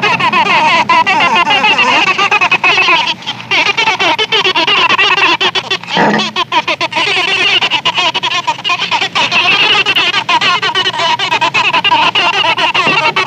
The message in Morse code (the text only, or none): none